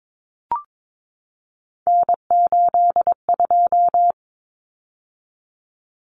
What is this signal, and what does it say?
Morse code, heard as a keyed tone, about 22 wpm, 700 Hz. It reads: N82
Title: Common Cut Numbers - 3-Digit 22wpm